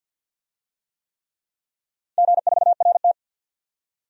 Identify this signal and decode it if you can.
Morse code, heard as a keyed tone, about 53 wpm, 700 Hz. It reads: G4RT
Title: DXCC-100 Call Signs - 4 Characters - 50wpm